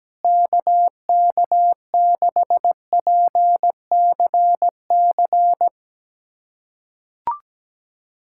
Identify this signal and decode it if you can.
Morse code, heard as a keyed tone, about 17 wpm, 700 Hz. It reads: KK6PCC
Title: USA Callsign DE Callsign 17wpm